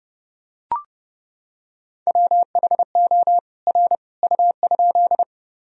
Morse code, heard as a keyed tone, about 32 wpm, 700 Hz. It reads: WHO R U?
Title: Sentences from Top 100 Words and Abbreviations 30wpm